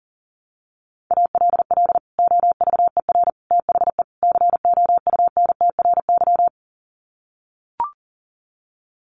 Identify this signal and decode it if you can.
Morse code, heard as a keyed tone, about 40 wpm, 700 Hz. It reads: ALL OVER THE COUNTRY
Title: Sets of 4 Words 40wpm